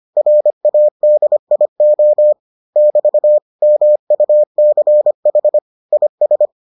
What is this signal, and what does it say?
Morse code, heard as a keyed tone, about 25 wpm, 600 Hz. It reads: RADIO <BT> MUCH IS